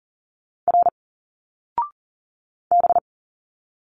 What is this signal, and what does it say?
Morse code, heard as a keyed tone, about 40 wpm, 700 Hz. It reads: R B
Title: Single Letters 40wpm